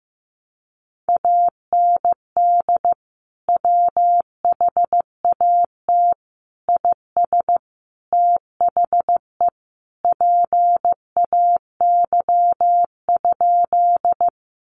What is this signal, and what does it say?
Morse code, heard as a keyed tone, about 15 wpm, 700 Hz. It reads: AND WHAT IS THE PAY?
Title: Sentences from Top 700 Words 15wpm